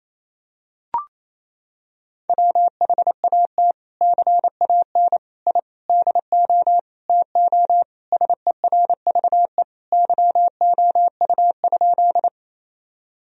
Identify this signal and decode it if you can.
Morse code, heard as a keyed tone, about 28 wpm, 700 Hz. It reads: WHAT CAN I DO TO SERVE YOU?
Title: Sentences from Top 1000 Words - Review Part 2 - 28wpm